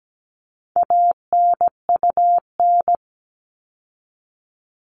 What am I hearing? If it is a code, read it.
Morse code, heard as a keyed tone, about 17 wpm, 700 Hz. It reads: ANUN
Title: All Cut Numbers - 4-Digit 17wpm